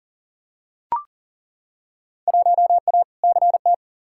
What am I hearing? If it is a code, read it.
Morse code, heard as a keyed tone, about 40 wpm, 700 Hz. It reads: 1A CT
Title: Field Day Exchanges Encore - 40wpm